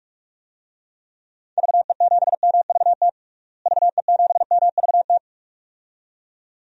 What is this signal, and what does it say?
Morse code, heard as a keyed tone, about 42 wpm, 700 Hz. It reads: VE7MVT VE7MVT
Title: DXCC-100 Call Signs - 6 Characters - Repeated 2x - 45wpm